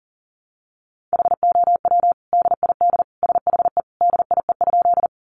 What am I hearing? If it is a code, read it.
Morse code, heard as a keyed tone, about 40 wpm, 700 Hz. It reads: HOW DID SHE DIE?
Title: Sentences from Top 1300 Words 40wpm